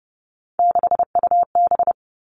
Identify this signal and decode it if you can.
Morse code, heard as a keyed tone, about 30 wpm, 700 Hz. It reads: 6UB
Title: All Cut Numbers - 3-Digit 30wpm